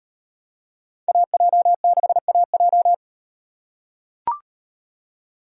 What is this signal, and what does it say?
Morse code, heard as a keyed tone, about 38 wpm, 700 Hz. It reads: AJ6AJ